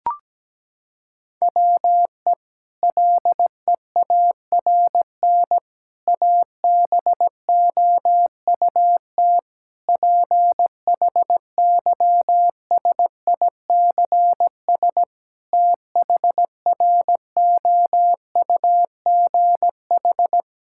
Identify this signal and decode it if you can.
Morse code, heard as a keyed tone, about 17 wpm, 700 Hz. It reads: WE LEARN ABOUT PHYSICS THROUGH
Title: Sentences from Top 1000 Words - Review Part 5 - 17wpm